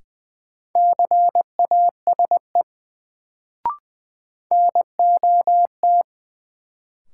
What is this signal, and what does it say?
Morse code, heard as a keyed tone, about 20 wpm, 700 Hz. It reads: CASE NOT
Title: Sets of 2 Words - New Word Drill - 20wpm